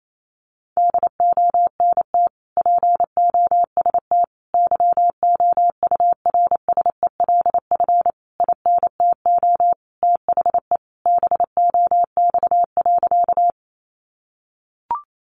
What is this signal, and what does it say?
Morse code, heard as a keyed tone, about 28 wpm, 700 Hz. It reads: DONT POST YOURSELF INTO THE BOX.
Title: Sentences from Top 1000 Words 28wpm